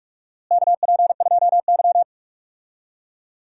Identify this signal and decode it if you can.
Morse code, heard as a keyed tone, about 42 wpm, 700 Hz. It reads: KP2Y
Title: DXCC-100 Call Signs - 4 Characters - Repeated 2x - 45wpm